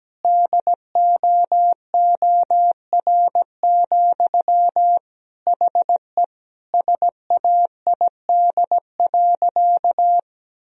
Morse code, heard as a keyed tone, about 17 wpm, 700 Hz. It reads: DOOR, HE SAID.
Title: Sentences from Top 1000 Words - Review Part 2 - 17wpm